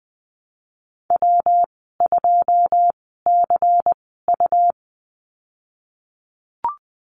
Morse code, heard as a keyed tone, about 20 wpm, spaced as for 15 wpm, 700 Hz. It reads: W2CU
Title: 128 Lesson 21g - Number 6 - Call Signs 20wpm